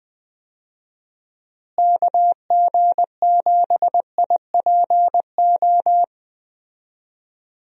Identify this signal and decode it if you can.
Morse code, heard as a keyed tone, about 20 wpm, 700 Hz. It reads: KG7IPO
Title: USA Call Signs - 6 Characters - Repeated 2x - 20wpm